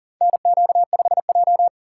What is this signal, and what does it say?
Morse code, heard as a keyed tone, about 40 wpm, 700 Hz. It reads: NQ5J
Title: USA Call Signs - 3x Character Spacing - 40wpm